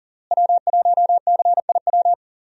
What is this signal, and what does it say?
Morse code, heard as a keyed tone, about 40 wpm, 700 Hz. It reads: W1CIW
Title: USA Call Signs - 5 Characters - Repeated 2x - 40wpm